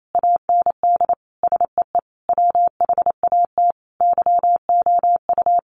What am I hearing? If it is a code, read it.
Morse code, heard as a keyed tone, about 28 wpm, 700 Hz. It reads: AND SEE WHAT YOU
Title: Sentences from Top 1300 Words 28wpm